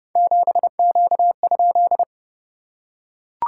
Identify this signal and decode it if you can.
Morse code, heard as a keyed tone, about 30 wpm, 690 Hz. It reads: 7Q?
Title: SOTA QSO Practice 30wpm